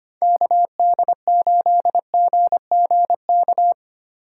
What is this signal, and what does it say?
Morse code, heard as a keyed tone, about 25 wpm, 700 Hz. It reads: KD8GGK
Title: USA Call Signs - 6 Characters - 4x Character Spacing - 25wpm